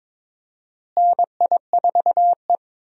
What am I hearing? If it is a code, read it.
Morse code, heard as a keyed tone, about 22 wpm, 700 Hz. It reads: NI4E
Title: USA Call Signs - 4 Characters - Repeated 2x - 22wpm